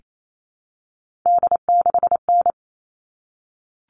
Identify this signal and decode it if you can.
Morse code, heard as a keyed tone, about 28 wpm, 700 Hz. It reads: D6N
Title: All Cut Numbers - 3-Digit 28wpm